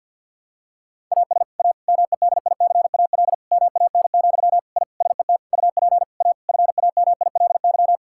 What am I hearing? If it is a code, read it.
Morse code, heard as a keyed tone, about 50 wpm, 700 Hz. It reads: AS A MEDICAL MAN, I SET UP A FAMILY
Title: Sentences from Top 1000 Words - Review Part 5 - 50wpm